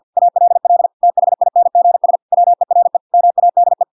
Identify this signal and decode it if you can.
Morse code, heard as a keyed tone, about 50 wpm, 700 Hz. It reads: ALL THINGS WERE MADE